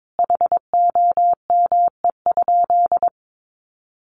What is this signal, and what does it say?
Morse code, heard as a keyed tone, about 22 wpm, 700 Hz. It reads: HOME?
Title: Sentences from Top 800 Words 22wpm